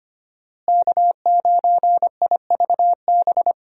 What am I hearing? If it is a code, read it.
Morse code, heard as a keyed tone, about 25 wpm, 700 Hz. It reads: K9IVB
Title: USA Call Signs - 5 Characters - Repeated 3x - 25wpm